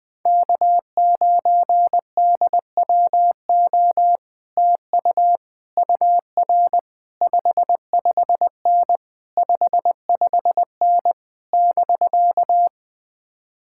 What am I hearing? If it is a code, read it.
Morse code, heard as a keyed tone, about 20 wpm, 700 Hz. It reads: K9DWO TU UR 55N 55N <BK>
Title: Speed-Racing Common QSO Phrases - 17wpm